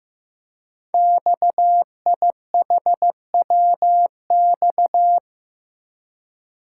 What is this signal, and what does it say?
Morse code, heard as a keyed tone, about 15 wpm, 700 Hz. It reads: XIHWX
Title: Code Groups - 5 Letters - 15wpm